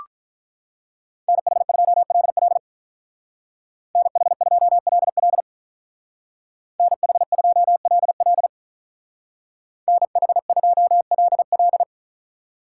Morse code, heard as a keyed tone, about 43 wpm, 700 Hz. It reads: NH2LL NH2LL NH2LL NH2LL
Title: Speed Racing - USA 2x2 Call Signs 35wpm